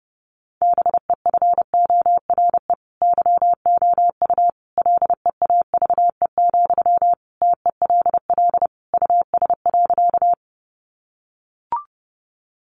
Morse code, heard as a keyed tone, about 30 wpm, 700 Hz. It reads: BEFORE YOU LEAVE, TELL US.
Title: Sentences from Top 400 Words 30wpm